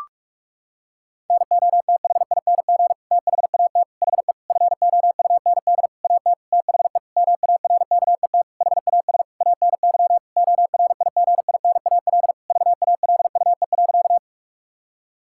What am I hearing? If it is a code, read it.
Morse code, heard as a keyed tone, about 41 wpm, 700 Hz. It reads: NOTHING THAT HE FOUND AT THE MARKET HAS ANY ORIGINAL VALUE.
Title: Sentences from Top 1000 Words - Review Part 6 - 45wpm